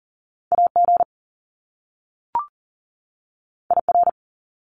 Morse code, heard as a keyed tone, about 40 wpm, 700 Hz. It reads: AG IR
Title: Top Two Letter Combinations in Words 40wpm